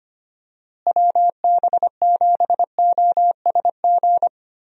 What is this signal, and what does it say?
Morse code, heard as a keyed tone, about 25 wpm, 700 Hz. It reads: WB7OSG
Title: USA Call Signs - 6 Characters - Repeated 3x - 25wpm